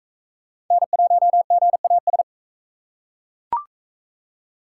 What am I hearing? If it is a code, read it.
Morse code, heard as a keyed tone, about 39 wpm, 700 Hz. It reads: N1GAS